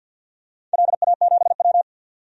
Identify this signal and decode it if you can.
Morse code, heard as a keyed tone, about 51 wpm, 700 Hz. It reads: LA7W
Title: DXCC-100 Call Signs - 4 Characters - 50wpm